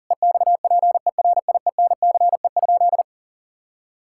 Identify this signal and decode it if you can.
Morse code, heard as a keyed tone, about 40 wpm, 700 Hz. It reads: EXPERIENCE?
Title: Sentences from Top 1000 Words - Review Part 5 - 40wpm